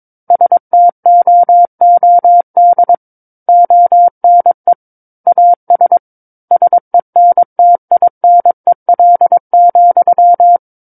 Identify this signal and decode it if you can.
Morse code, heard as a keyed tone, about 22 wpm, 700 Hz. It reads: STOOD ONE AS SENTINEL,
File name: Book One / 13-Chapter 13 - 22 WPM